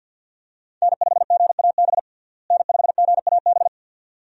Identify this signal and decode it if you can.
Morse code, heard as a keyed tone, about 52 wpm, 700 Hz. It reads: N5GAB N5GAB
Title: USA Call Signs - 5 Characters - Repeated 2x - 50wpm